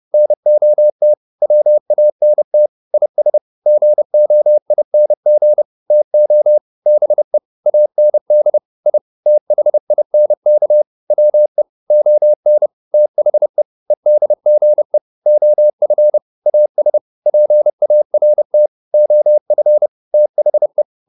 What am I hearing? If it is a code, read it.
Morse code, heard as a keyed tone, about 30 wpm, 600 Hz. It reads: NOT WANT IS GOING TO BE AND I THINK WE ON THE EDGE OF AS PART OF THE
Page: advanced-practice-files